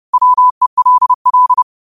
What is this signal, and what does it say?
Morse code, heard as a keyed tone, about 30 wpm, 1000 Hz. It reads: WELL
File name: qotdfiles / 30wpm